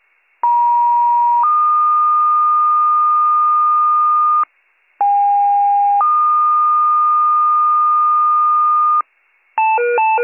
Two-tone page, heard as a one-second tone, then a three-second tone, about 950 then 1220 Hz, about 810 then 1220 Hz. It is heard from 0.4 to 9.0 s.